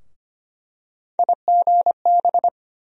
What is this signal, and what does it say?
Morse code, heard as a keyed tone, about 25 wpm, 700 Hz. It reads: IGB
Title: USA Call Sign Postfixes - 3 Letters 25wpm